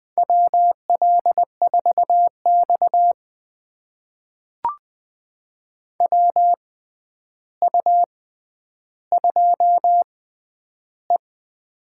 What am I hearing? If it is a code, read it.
Morse code, heard as a keyed tone, about 21 wpm, spaced as for 10 wpm, 700 Hz. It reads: WL4X WU2E
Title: USA Call Signs - 4 Characters - 6x Character Spacing - 20wpm